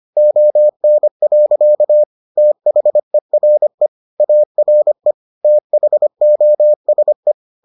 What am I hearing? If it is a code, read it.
Morse code, heard as a keyed tone, about 25 wpm, 600 Hz. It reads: ON. THERE ARE THOSE